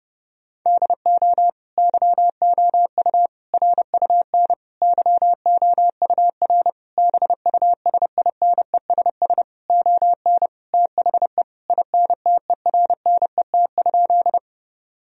Morse code, heard as a keyed tone, about 32 wpm, 700 Hz. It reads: DO YOU RUN YOUR BUSINESS ON THE INTERNET?
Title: Sentences from Top 1000 Words - Review Part 4 - 30wpm